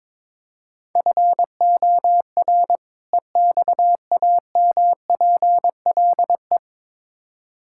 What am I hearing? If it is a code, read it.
Morse code, heard as a keyed tone, about 21 wpm, 700 Hz. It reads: FOR EXAMPLE
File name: Build A Phrase - 2-Word Phrases - 22wpm